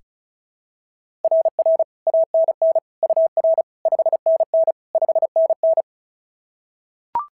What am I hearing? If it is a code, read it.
Morse code, heard as a keyed tone, about 36 wpm, 690 Hz. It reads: RR ANN UR 5NN 5NN
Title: SOTA QSO Practice 35wpm